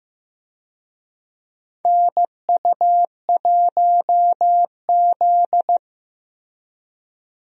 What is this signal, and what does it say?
Morse code, heard as a keyed tone, about 15 wpm, 700 Hz. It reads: NU1Z
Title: USA Call Signs - 4 Characters - Repeated 2x - 15wpm